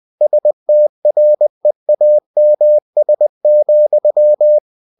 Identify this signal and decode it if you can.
Morse code, heard as a keyed tone, about 20 wpm, 600 Hz. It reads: STREAMS,
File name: cw-books / winnie-the-pooh / winnie-20wpm-page-0028